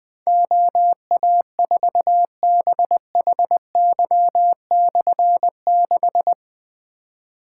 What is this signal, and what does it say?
Morse code, heard as a keyed tone, about 20 wpm, 700 Hz. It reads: OA4BHY/6
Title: Difficult Call Signs 20wpm